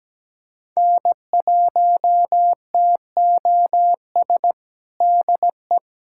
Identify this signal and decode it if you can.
Morse code, heard as a keyed tone, about 17 wpm, 700 Hz. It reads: N1TOS DE
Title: USA Callsign DE Callsign - 5 Characters - 17wpm